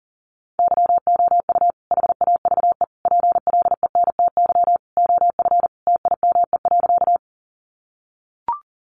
Morse code, heard as a keyed tone, about 40 wpm, 700 Hz. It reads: YOU HAVE PLENTY OF TIME.
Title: Sentences from Top 1300 Words 40wpm